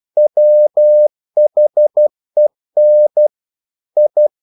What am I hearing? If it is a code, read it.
Morse code, heard as a keyed tone, about 12 wpm, 600 Hz. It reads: WHEN I
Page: English quotations in CW (telegraphy - morse)